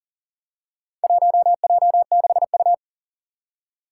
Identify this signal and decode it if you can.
Morse code, heard as a keyed tone, about 40 wpm, 700 Hz. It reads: 1J6U
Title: Four Challenging Letter-Number Combinations 40wpm